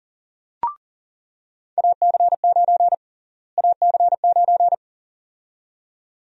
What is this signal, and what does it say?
Morse code, heard as a keyed tone, about 40 wpm, 700 Hz. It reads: AC9 AC9
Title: USA Call Sign Prefixes - Repeated 2x - 40wpm